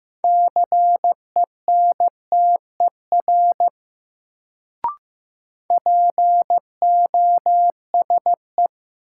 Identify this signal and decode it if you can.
Morse code, heard as a keyed tone, about 16 wpm, 700 Hz. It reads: CENTER POSE
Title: Sentences from Top 1000 Words - Review Part 2 - New Word Drill - 15wpm